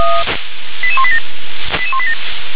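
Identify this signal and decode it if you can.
Five-tone selective call (ZVEI), heard as quick runs of five tones, tones of about 70 ms. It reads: 88187 88187
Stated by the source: Received on an SDR; ZVEI five-tone call 88187.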